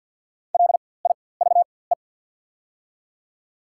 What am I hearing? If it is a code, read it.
Morse code, heard as a keyed tone, about 48 wpm, spaced as for 23 wpm, 700 Hz. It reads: LIVE
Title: 4x Character Spacing - Top 500 Words - 50wpm